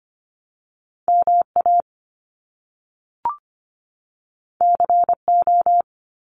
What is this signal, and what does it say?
Morse code, heard as a keyed tone, about 25 wpm, 700 Hz. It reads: MA CO
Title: US States 25wpm